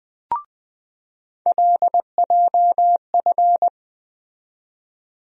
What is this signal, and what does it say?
Morse code, heard as a keyed tone, about 20 wpm, 700 Hz. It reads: LJF
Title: USA Call Sign Postfixes - 3 Letters 20wpm